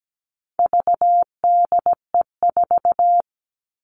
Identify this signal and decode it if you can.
Morse code, heard as a keyed tone, about 17 wpm, 700 Hz. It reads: VDE4
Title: All Cut Numbers - 4-Digit 17wpm